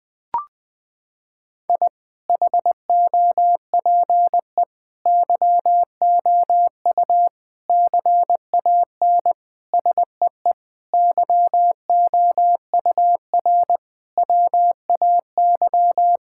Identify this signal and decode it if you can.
Morse code, heard as a keyed tone, about 20 wpm, 700 Hz. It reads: I HOPE YOU CAN SEE YOUR WAY
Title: Sets of 8 Words - 20wpm